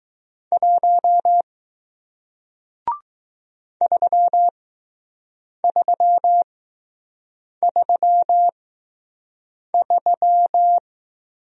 Morse code, heard as a keyed tone, about 20 wpm, spaced as for 6 wpm, 700 Hz. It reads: 1 3333